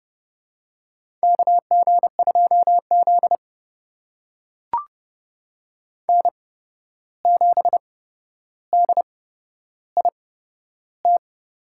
Morse code, heard as a keyed tone, about 31 wpm, spaced as for 12 wpm, 700 Hz. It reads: KG2Z N7DIT
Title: USA Call Signs - 8x Character Spacing - 30wpm